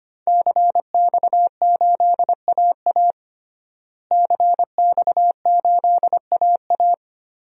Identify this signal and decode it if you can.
Morse code, heard as a keyed tone, about 25 wpm, 700 Hz. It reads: CX8AA CX8AA
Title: DXCC-100 Call Signs - 5 Characters - Repeated 3x - 25wpm